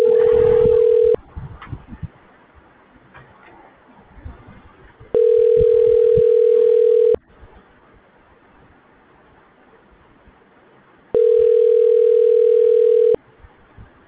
Ringback tone (North America), 440 + 480 Hz.